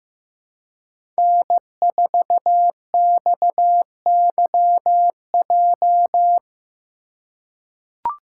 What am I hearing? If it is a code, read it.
Morse code, heard as a keyed tone, about 15 wpm, 700 Hz. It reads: N4XYJ
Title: USA Call Signs - 5 Characters - 4x Character Spacing - 15wpm